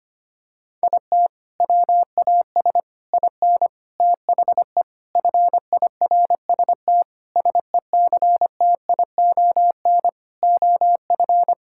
Morse code, heard as a keyed tone, about 25 wpm, 700 Hz. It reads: IT WAS IN THE FIRST SECTION OF
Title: Sentences from Top 1000 Words - Review Part 5 - 25wpm